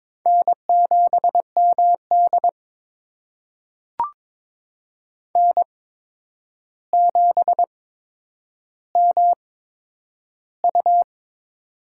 Morse code, heard as a keyed tone, about 22 wpm, spaced as for 9 wpm, 700 Hz. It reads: N7MD N7MU
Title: USA Call Signs - 4 Characters - 8x Character Spacing - 22wpm